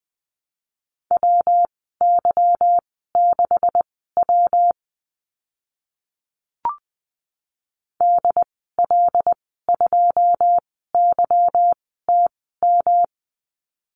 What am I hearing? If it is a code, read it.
Morse code, heard as a keyed tone, about 20 wpm, spaced as for 15 wpm, 700 Hz. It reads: WY6W DL2YTM
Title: 154 Lesson 25g - Letter p - Call Signs 20wpm